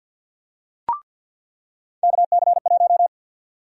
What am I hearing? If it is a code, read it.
Morse code, heard as a keyed tone, about 50 wpm, 700 Hz. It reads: KC1
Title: USA Call Sign Prefixes - 50wpm